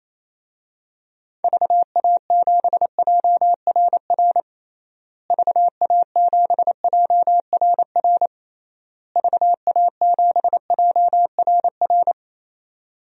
Morse code, heard as a keyed tone, about 27 wpm, 700 Hz. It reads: VA7JRR VA7JRR VA7JRR
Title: DXCC-100 Call Signs - 6 Characters - Repeated 3x - 28wpm